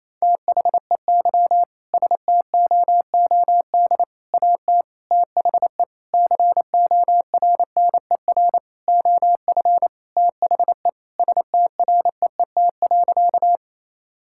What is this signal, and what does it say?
Morse code, heard as a keyed tone, about 28 wpm, 700 Hz. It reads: THEY STOOD AT THE CORNER OF THE STREET.
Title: Sentences from Top 1600 Words 28wpm